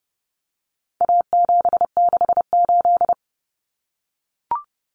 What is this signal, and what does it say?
Morse code, heard as a keyed tone, about 30 wpm, 700 Hz. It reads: A768
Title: Common Cut Numbers - 4-Digit 30wpm